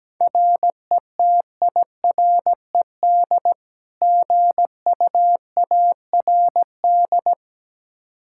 Morse code, heard as a keyed tone, about 17 wpm, 700 Hz. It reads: RETIRED GUARD
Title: Practice QSO - Exchange 3 - More Info - 17wpm